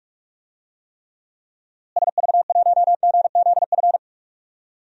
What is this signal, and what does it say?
Morse code, heard as a keyed tone, about 46 wpm, 700 Hz. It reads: SV1GZF
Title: DXCC-100 Call Signs - 6 Characters - Repeated 3x - 45wpm